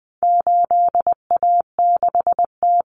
Morse code, heard as a keyed tone, about 20 wpm, 700 Hz. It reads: 8A6T
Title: Common Cut Numbers - 4-Digit 20wpm